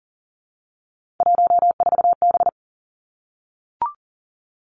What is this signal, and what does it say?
Morse code, heard as a keyed tone, about 40 wpm, 700 Hz. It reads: 14B
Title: Three Challenging Letter-Number Combinations 40wpm